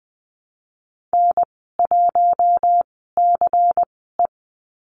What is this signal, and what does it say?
Morse code, heard as a keyed tone, about 20 wpm, spaced as for 15 wpm, 700 Hz. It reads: N1CE
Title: 179 Lesson 29g - Slash - Call Signs 20wpm